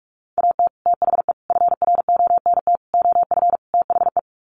Morse code, heard as a keyed tone, about 44 wpm, 700 Hz. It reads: AT THE FRONT OF THE
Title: Sets of 5 Words 45wpm